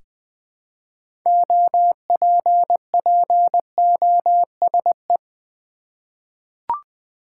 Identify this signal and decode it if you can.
Morse code, heard as a keyed tone, about 20 wpm, 700 Hz. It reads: OPPOSE
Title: 6 Letter Words - First 3 Letters Given - 20wpm